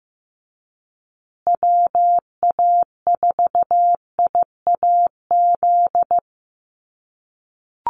Morse code, heard as a keyed tone, about 15 wpm, 700 Hz. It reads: WA4IAZ
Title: USA Call Signs Repeated 3x 15wpm